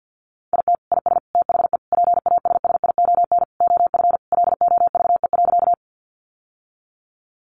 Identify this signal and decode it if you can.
Morse code, heard as a keyed tone, about 50 wpm, 700 Hz. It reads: IT IS THE PASSION OF LOVE.
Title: Sentences from Top 1300 Words 50wpm